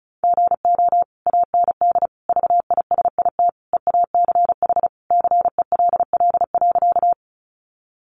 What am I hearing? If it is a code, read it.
Morse code, heard as a keyed tone, about 34 wpm, 700 Hz. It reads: GO AND VISIT EACH CELL.
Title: Sentences from Top 1100 Words 35wpm